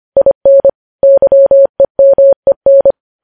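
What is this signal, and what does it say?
Morse code, heard as a keyed tone, about 25 wpm, 550 Hz. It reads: IN YEMEN
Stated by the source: Morse code practice of news headlines at 25 wpm, opening with vvv.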